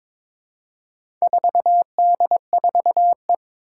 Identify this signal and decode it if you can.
Morse code, heard as a keyed tone, about 22 wpm, 700 Hz. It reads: 4D4E
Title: Four Challenging Letter-Number Combinations 22wpm